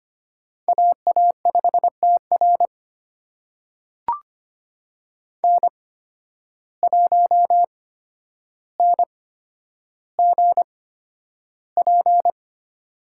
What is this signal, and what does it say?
Morse code, heard as a keyed tone, about 25 wpm, spaced as for 11 wpm, 700 Hz. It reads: AA5TR N1NGP
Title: USA Call Signs - 8x Character Spacing - 25wpm